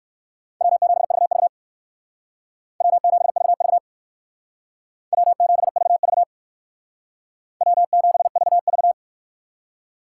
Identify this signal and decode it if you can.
Morse code, heard as a keyed tone, about 67 wpm, 700 Hz. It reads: W7VV W7VV W7VV W7VV